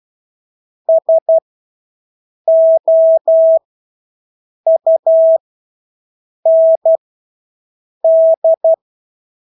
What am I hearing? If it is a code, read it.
Morse code, heard as a keyed tone, about 12 wpm, spaced as for 6 wpm, 650 Hz. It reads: SOUND
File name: Engelske ord 12-6 wpm